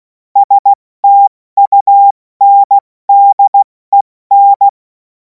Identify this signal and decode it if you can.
Morse code, heard as a keyed tone, about 16 wpm, 800 Hz. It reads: STUNDEN
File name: Morsekurs / Witze / Witz 11. August 2026 70 F80